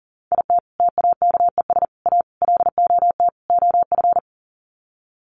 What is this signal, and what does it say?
Morse code, heard as a keyed tone, about 40 wpm, 700 Hz. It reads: IT TAKES A LOT OF
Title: Sets of 5 Words Encore 40wpm